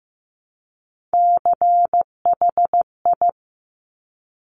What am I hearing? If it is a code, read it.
Morse code, heard as a keyed tone, about 15 wpm, 700 Hz. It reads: CHI
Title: Top Three Letter Combinations in Words 15wpm